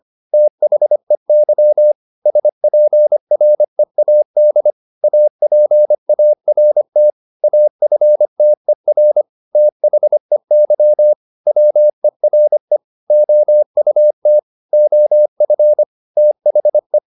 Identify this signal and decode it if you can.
Morse code, heard as a keyed tone, about 25 wpm, 600 Hz. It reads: THEY SPREAD APART AFTER THEY WERE OUT OF THE
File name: the old man and the sea part 1 25wpm